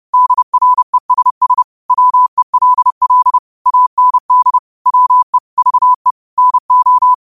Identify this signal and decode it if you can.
Morse code, heard as a keyed tone, about 30 wpm, 1000 Hz. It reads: DRESS WELL AND WE VE NO
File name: qotdfiles / 30wpm